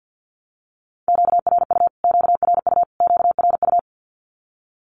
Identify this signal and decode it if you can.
Morse code, heard as a keyed tone, about 50 wpm, 700 Hz. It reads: QRU QRU QRU
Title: Q-Codes Repeated 3x 50wpm